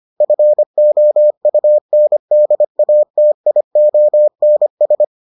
Morse code, heard as a keyed tone, about 25 wpm, 600 Hz. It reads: FOUNDATIONS